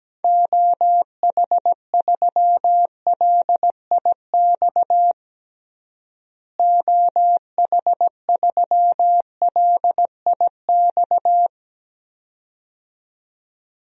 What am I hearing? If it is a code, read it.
Morse code, heard as a keyed tone, about 17 wpm, 700 Hz. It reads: OH3LIX OH3LIX
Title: DXCC-100 Call Signs - 6 Characters - Repeated 2x - 17wpm